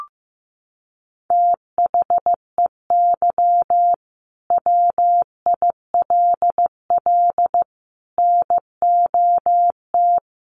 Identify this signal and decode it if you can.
Morse code, heard as a keyed tone, about 15 wpm, 700 Hz. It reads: THEY WILL NOT
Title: Sentences from Top 1500 Words 15wpm